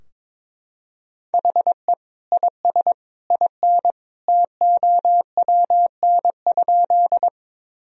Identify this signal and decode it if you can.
Morse code, heard as a keyed tone, about 21 wpm, 700 Hz. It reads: HE IS IN TOWN?
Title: Sentences from Top 1000 Words - Review Part 2 - 22wpm